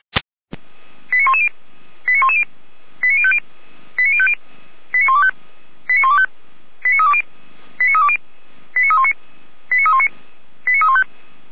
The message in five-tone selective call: 89110 89110 89050 89050 89125 89125 89320 89320 89319 89319 89315

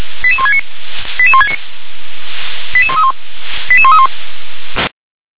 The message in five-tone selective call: 88169 88169 88131 88131

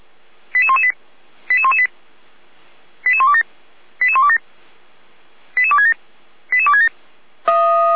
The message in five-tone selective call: 88118 88118 88127 88127 88267 88267

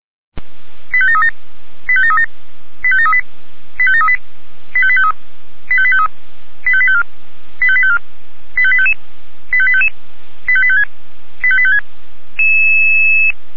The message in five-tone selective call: 85737 85737 85738 85738 85853 85853 85854 85854 85855 85855 85856 85856

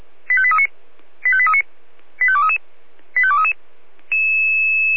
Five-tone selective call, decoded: 85839 85839 85233 85233

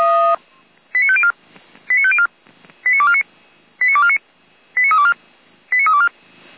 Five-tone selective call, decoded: 89504 89504 89249 89249 89324 89324

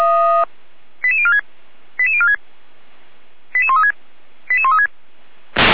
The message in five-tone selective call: 88046 88046 88136 88136

